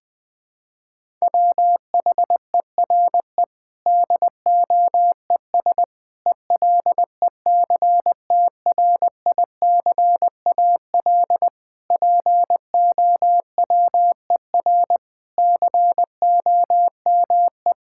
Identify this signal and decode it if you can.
Morse code, heard as a keyed tone, about 20 wpm, 700 Hz. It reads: WHERE DOES ELECTRICAL POWER COME
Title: Sentences from Top 1000 Words - Review Part 6 - 20wpm